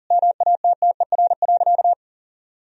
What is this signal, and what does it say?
Morse code, heard as a keyed tone, about 40 wpm, 700 Hz. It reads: MATTER.
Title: Sentences from Top 1000 Words - Review Part 4 - 40wpm